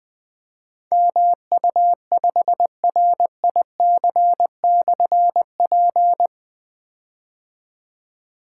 Morse code, heard as a keyed tone, about 21 wpm, 700 Hz. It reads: MU5RIC/P
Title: Difficult Call Signs 20wpm